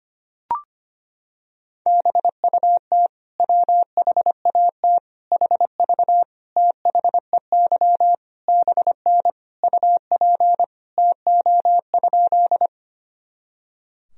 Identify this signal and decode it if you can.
Morse code, heard as a keyed tone, about 25 wpm, 700 Hz. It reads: BUT WHAT HV THEY BN UP TO?
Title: Sentences from Top 100 Words and Abbreviations 25wpm